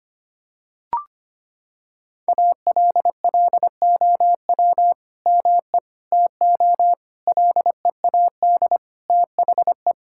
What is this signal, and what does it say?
Morse code, heard as a keyed tone, about 25 wpm, 700 Hz. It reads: ALLOW ME TO LEAD THE
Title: Sentences from Top 1000 Words - Review Part 5 - 25wpm